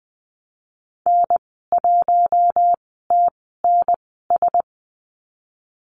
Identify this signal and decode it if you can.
Morse code, heard as a keyed tone, about 20 wpm, spaced as for 14 wpm, 700 Hz. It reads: N1TNS
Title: 042 Lesson 8g - Number 1 - Call Signs 20wpm